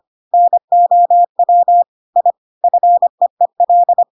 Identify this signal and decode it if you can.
Morse code, heard as a keyed tone, about 25 wpm, 700 Hz. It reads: NOW I FEEL
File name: Pinocchio / Pinocchio ch7 25 25 ik1hge